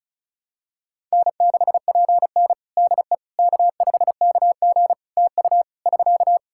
Morse code, heard as a keyed tone, about 36 wpm, 700 Hz. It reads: N6PN DE K5KG TU <SK>
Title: Practice QSO - Exchange 4 - Ending - 35wpm